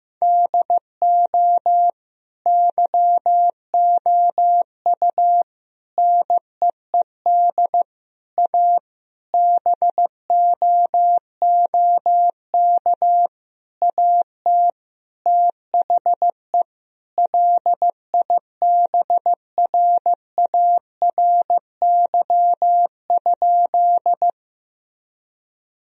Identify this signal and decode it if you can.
Morse code, heard as a keyed tone, about 15 wpm, 700 Hz. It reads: DO YOU NEED A BOOK AT THE LIBRARY?
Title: Sentences from Top 1000 Words - Review Part 4 - 15wpm